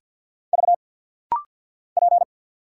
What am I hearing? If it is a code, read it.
Morse code, heard as a keyed tone, about 50 wpm, 700 Hz. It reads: V P
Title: Single Letter-Number Alt - Rapid-Fire 50wpm